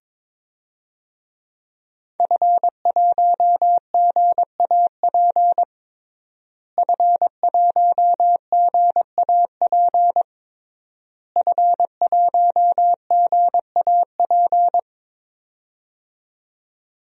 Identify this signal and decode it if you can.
Morse code, heard as a keyed tone, about 22 wpm, 700 Hz. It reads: F1GAP F1GAP F1GAP
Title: DXCC-100 Call Signs - 5 Characters - Repeated 3x - 22wpm